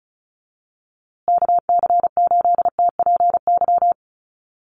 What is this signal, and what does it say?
Morse code, heard as a keyed tone, about 34 wpm, 700 Hz. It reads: KC8TPY
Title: USA Call Signs Repeated 3x 35wpm